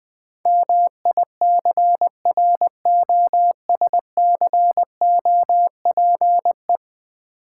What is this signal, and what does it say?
Morse code, heard as a keyed tone, about 20 wpm, 700 Hz. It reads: MICROSCOPE
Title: Easy Words with Hints - 20wpm